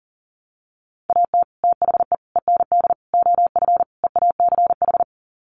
Morse code, heard as a keyed tone, about 37 wpm, 700 Hz. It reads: AT THE END OF EACH
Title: Sets of 5 Words Encore 40wpm